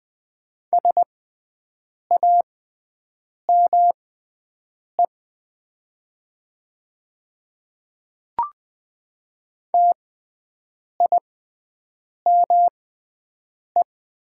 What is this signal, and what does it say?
Morse code, heard as a keyed tone, about 20 wpm, spaced as for 7 wpm, 700 Hz. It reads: SAME TIME
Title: 6x Character Spacing - Top 200 Words - 20wpm